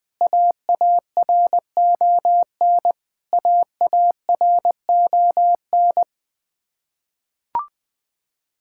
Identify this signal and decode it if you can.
Morse code, heard as a keyed tone, about 20 wpm, 700 Hz. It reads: AARON AARON
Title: Practice QSO - Exchange 1 - Beginning - 20wpm